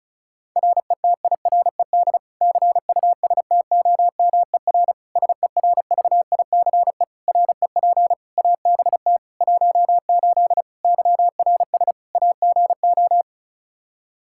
Morse code, heard as a keyed tone, about 36 wpm, 700 Hz. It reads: RETIRED CUSTOMER SERVICE REP ABT 18 YRS AGO
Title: Practice QSO - Exchange 3 - More Info - 35wpm